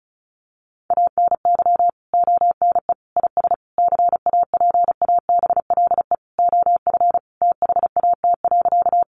Morse code, heard as a keyed tone, about 34 wpm, 700 Hz. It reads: ANY ONE IS CAPABLE OF THAT.
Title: Sentences from Top 1200 Words 35wpm